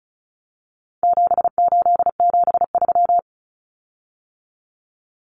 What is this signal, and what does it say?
Morse code, heard as a keyed tone, about 34 wpm, 700 Hz. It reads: 7873
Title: Common Cut Numbers - 4-Digit 35wpm